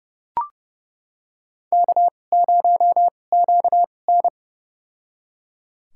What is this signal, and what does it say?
Morse code, heard as a keyed tone, about 32 wpm, spaced as for 22 wpm, 700 Hz. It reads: K0QN
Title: USA Call Signs - 2x Character Spacing - 30wpm